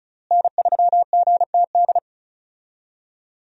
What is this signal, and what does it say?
Morse code, heard as a keyed tone, about 34 wpm, 700 Hz. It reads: N3GTD
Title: USA Call Signs - 5 Characters - Repeated 2x - 35wpm